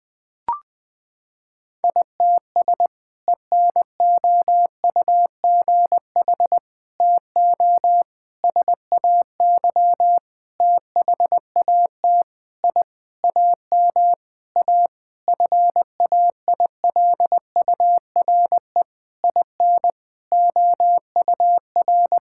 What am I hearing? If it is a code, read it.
Morse code, heard as a keyed tone, about 20 wpm, 700 Hz. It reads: ITS ENOUGH TO SAY THAT I AM A FAILURE IN OUR
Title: Sentences from Top 1000 Words - Review Part 6 - 20wpm